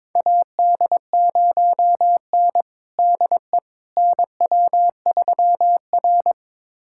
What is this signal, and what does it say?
Morse code, heard as a keyed tone, about 22 wpm, 700 Hz. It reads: AD0N DE NW3R
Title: USA Callsign DE Callsign - 4 Characters - 22wpm